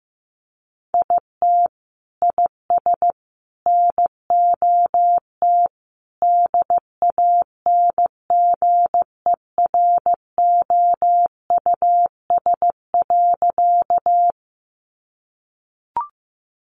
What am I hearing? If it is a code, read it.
Morse code, heard as a keyed tone, about 15 wpm, 700 Hz. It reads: IT IS NOT DANGEROUS.
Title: Sentences from Top 1200 Words 15wpm